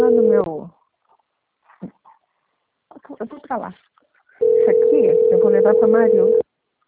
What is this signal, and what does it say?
Ringback tone (North America), 440 + 480 Hz, with speech.
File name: out- - - - - .1444